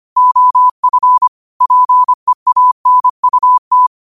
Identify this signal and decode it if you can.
Morse code, heard as a keyed tone, about 25 wpm, 1000 Hz. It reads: OF PEANUT
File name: qotdfiles / 25wpm